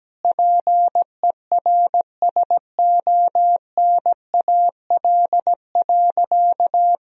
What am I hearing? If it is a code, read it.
Morse code, heard as a keyed tone, about 17 wpm, 700 Hz. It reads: PERSONAL.
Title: Sentences from Top 1000 Words - Review Part 4 - 17wpm